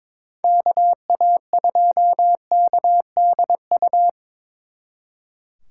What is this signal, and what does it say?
Morse code, heard as a keyed tone, about 25 wpm, 700 Hz. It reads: KA2KDU
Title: Speed Racing - USA 2x3 Call Signs 22wpm